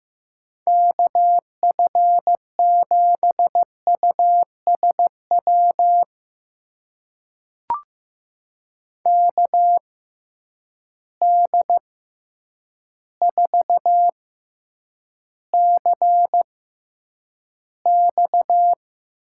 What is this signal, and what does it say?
Morse code, heard as a keyed tone, about 15 wpm, spaced as for 8 wpm, 700 Hz. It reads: KF7USW KD4CX
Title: USA Call Signs - 6 Characters - 6x Character Spacing - 15wpm